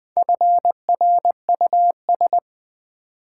Morse code, heard as a keyed tone, about 20 wpm, 700 Hz. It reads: FRUS
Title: Build A Word - 4 Letters At A Time - 11-Letter Words - 20wpm